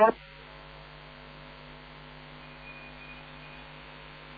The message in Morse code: none

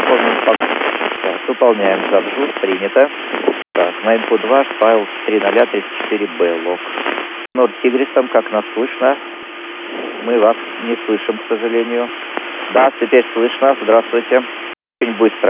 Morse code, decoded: none

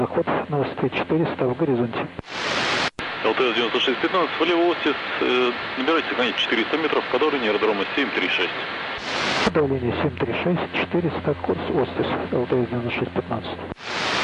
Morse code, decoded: none

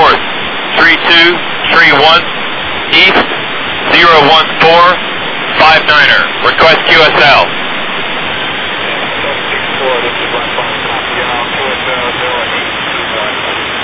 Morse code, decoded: none

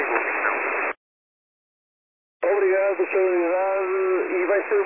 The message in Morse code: none

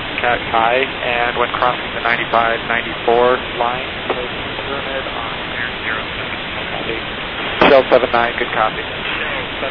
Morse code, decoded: none

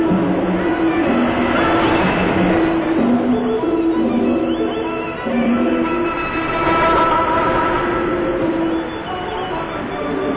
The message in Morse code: none

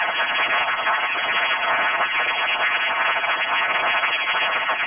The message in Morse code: none

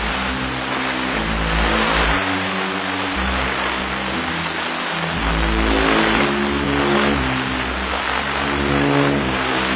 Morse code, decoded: none

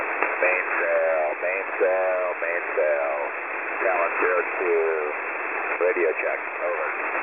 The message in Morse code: none

